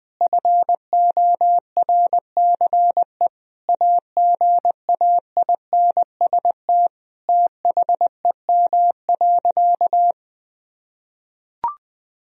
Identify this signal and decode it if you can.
Morse code, heard as a keyed tone, about 20 wpm, 700 Hz. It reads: FORCE AGAINST THEM.
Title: Sentences from Top 1000 Words - Review Part 4 - 20wpm